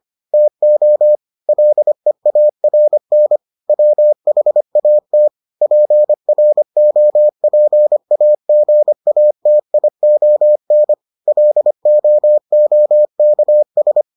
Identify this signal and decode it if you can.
Morse code, heard as a keyed tone, about 25 wpm, 600 Hz. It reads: TO LEARN WHAT PROPAGATION LOOKS